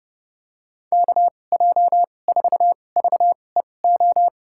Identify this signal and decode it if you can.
Morse code, heard as a keyed tone, about 31 wpm, spaced as for 22 wpm, 700 Hz. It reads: KJ4VEO
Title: USA Call Signs - 2x Character Spacing - 30wpm